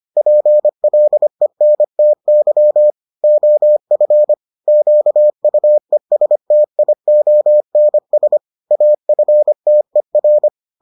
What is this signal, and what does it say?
Morse code, heard as a keyed tone, about 25 wpm, 600 Hz. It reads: PLENTY OF QUESTIONS AFTER